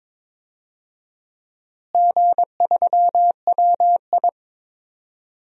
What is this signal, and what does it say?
Morse code, heard as a keyed tone, about 22 wpm, 700 Hz. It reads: G3WI
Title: DXCC-100 Call Signs - 4 Characters - Repeated 2x - 22wpm